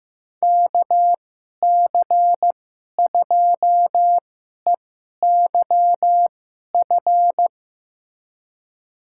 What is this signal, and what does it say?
Morse code, heard as a keyed tone, about 15 wpm, spaced as for 11 wpm, 700 Hz. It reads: KC2EYF
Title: USA Call Signs - 6 Characters - 2x Character Spacing - 15wpm